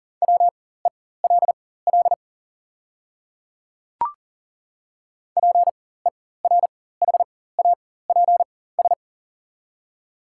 Morse code, heard as a keyed tone, about 40 wpm, spaced as for 19 wpm, 690 Hz. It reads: WELL PERHAPS
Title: 4x Character Spacing - Top 500 Words - 40wpm